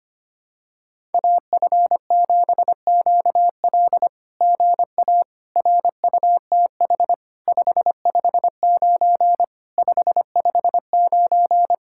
Morse code, heard as a keyed tone, about 24 wpm, 700 Hz. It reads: AF7QL GA RUTH 559 559
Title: SOTA QSO Practice 25wpm